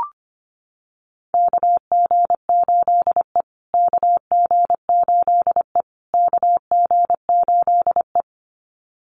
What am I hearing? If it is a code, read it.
Morse code, heard as a keyed tone, about 25 wpm, 700 Hz. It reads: KG8E KG8E KG8E
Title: USA Call Signs Repeated 3x 25wpm